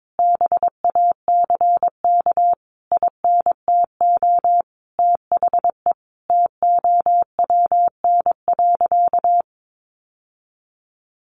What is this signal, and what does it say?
Morse code, heard as a keyed tone, about 22 wpm, 700 Hz. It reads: BACK INTO THE TOWN.
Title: Sentences from Top 1000 Words 22wpm